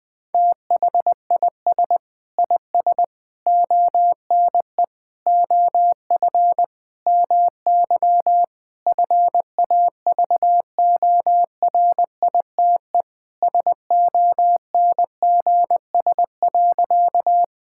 Morse code, heard as a keyed tone, about 20 wpm, 700 Hz. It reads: THIS IS ONE OF MY FAVORITE SONGS.
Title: Sentences with 4-Word Phrases - 20wpm